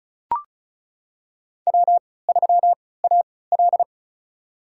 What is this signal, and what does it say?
Morse code, heard as a keyed tone, about 36 wpm, spaced as for 20 wpm, 700 Hz. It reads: W3AL
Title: USA Call Signs - 4 Characters - 3x Character Spacing - 35wpm